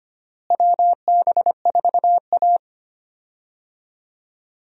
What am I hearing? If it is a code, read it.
Morse code, heard as a keyed tone, about 25 wpm, 700 Hz. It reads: WB4A
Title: USA Call Signs - 4 Characters - Repeated 3x - 25wpm